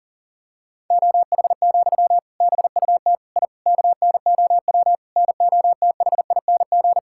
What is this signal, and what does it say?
Morse code, heard as a keyed tone, about 40 wpm, 700 Hz. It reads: OH, BUT I KNOW NOTHING
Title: Sentences from Top 1000 Words - Review Part 6 - 40wpm